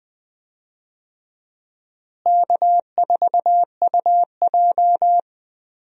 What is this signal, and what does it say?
Morse code, heard as a keyed tone, about 20 wpm, 700 Hz. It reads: K4UJ
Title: USA Call Signs - 4 Characters - Repeated 3x - 20wpm